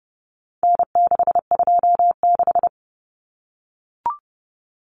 Morse code, heard as a keyed tone, about 30 wpm, 700 Hz. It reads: N626
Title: Common Cut Numbers - 4-Digit 30wpm